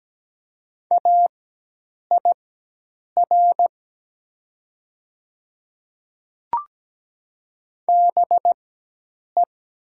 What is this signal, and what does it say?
Morse code, heard as a keyed tone, about 17 wpm, spaced as for 8 wpm, 700 Hz. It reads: AIR BE
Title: 4x Character Spacing - Top 200 Words - 17wpm